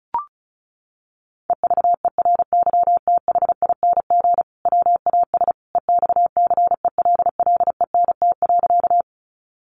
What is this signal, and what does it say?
Morse code, heard as a keyed tone, about 34 wpm, 700 Hz. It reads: EVERYTHING WAS EXCELLENT.
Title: Sentences from Top 1000 Words 35wpm